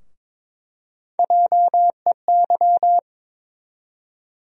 Morse code, heard as a keyed tone, about 22 wpm, 700 Hz. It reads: JEY
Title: USA Call Sign Postfixes - 3 Letters 22wpm